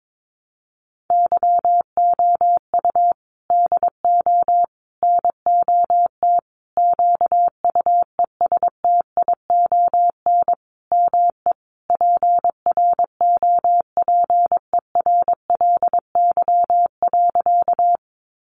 Morse code, heard as a keyed tone, about 22 wpm, 700 Hz. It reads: YOU DO NOT QUESTION ME PROPERLY.
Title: Sentences from Top 1100 Words 22wpm